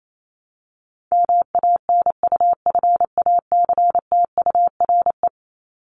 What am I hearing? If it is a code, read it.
Morse code, heard as a keyed tone, about 28 wpm, 700 Hz. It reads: MANUFACTURE
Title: ING Suffix - Top 100 Words with Base Words 28wpm